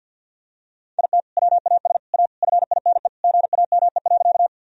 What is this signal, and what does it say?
Morse code, heard as a keyed tone, about 51 wpm, 700 Hz. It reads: IT WAS A FINE GAME.
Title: Sentences from Top 1000 Words - Review Part 3 - 50wpm